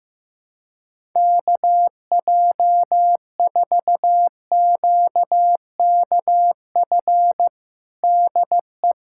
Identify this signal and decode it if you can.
Morse code, heard as a keyed tone, about 15 wpm, 700 Hz. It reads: KJ4QKF DE
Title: USA Callsign DE Callsign - 6 Characters - 15wpm